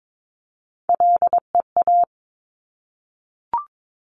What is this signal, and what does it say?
Morse code, heard as a keyed tone, about 22 wpm, 700 Hz. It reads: LEA
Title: Top Three Letter Combinations in Words 22wpm